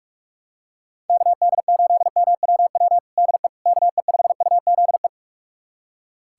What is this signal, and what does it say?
Morse code, heard as a keyed tone, about 45 wpm, 700 Hz. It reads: KD8MWW DE KE5UZE
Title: USA Callsign DE Callsign - 6 Characters - 45wpm